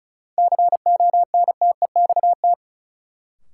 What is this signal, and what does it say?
Morse code, heard as a keyed tone, about 34 wpm, 700 Hz. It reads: CONTEXT
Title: Sentences from Top 1000 Words - Review Part 7 - New Word Drill - 35wpm